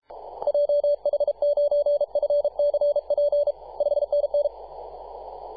Morse code, heard as a keyed tone, about 33 wpm, 580 Hz. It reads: JH9FCP 5NN
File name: posts / 2015-sept-24-2118z-t2gc-18-071-cw